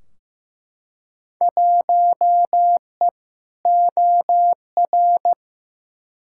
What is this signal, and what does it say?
Morse code, heard as a keyed tone, about 15 wpm, 700 Hz. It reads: 1E OR
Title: Field Day Exchanges Encore - 15wpm